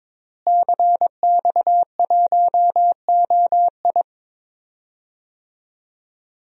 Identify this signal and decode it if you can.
Morse code, heard as a keyed tone, about 22 wpm, 700 Hz. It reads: CX1OI
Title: DXCC-100 Call Signs - 5 Characters - Repeated 2x - 22wpm